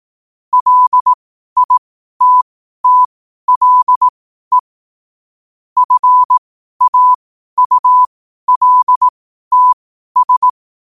Morse code, heard as a keyed tone, about 18 wpm, spaced as for 12 wpm, 1000 Hz. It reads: LITTLE FAULTS
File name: qotdfiles / 12wpm